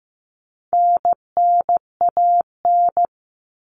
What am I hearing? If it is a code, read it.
Morse code, heard as a keyed tone, about 15 wpm, 700 Hz. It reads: NNAN
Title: All Cut Numbers - 4-Digit 15wpm